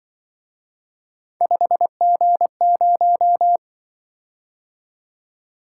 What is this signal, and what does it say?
Morse code, heard as a keyed tone, about 23 wpm, 700 Hz. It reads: 5G0
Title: Speed-Racing - Three Letter-Number Combinations - 20wpm